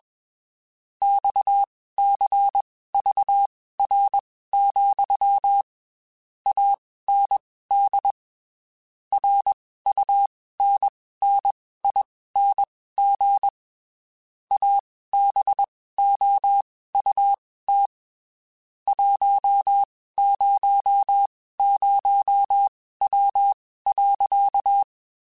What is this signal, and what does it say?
Morse code, heard as a keyed tone, about 21 wpm, spaced as for 15 wpm, 800 Hz. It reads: XCVR, AND RUNNING ABOUT 100W.
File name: QSOs / QSOs-mp3s / WD0FBT 1@20